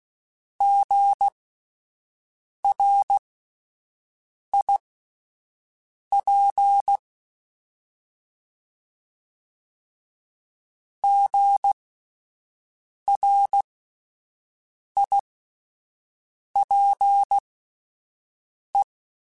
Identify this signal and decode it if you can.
Morse code, heard as a keyed tone, about 16 wpm, spaced as for 5 wpm, 800 Hz. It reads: GRIP GRIPE